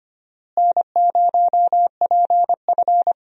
Morse code, heard as a keyed tone, about 25 wpm, 700 Hz. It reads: N0PF
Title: USA Call Signs - 4 Characters - Repeated 2x - 25wpm